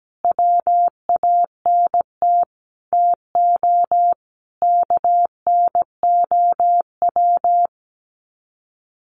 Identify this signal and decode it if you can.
Morse code, heard as a keyed tone, about 17 wpm, 700 Hz. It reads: WANT TO KNOW
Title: Sets of 5 Words Encore2 17wpm